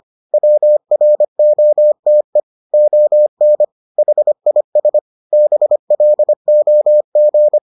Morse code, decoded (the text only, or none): WROTE ON HIS BLOG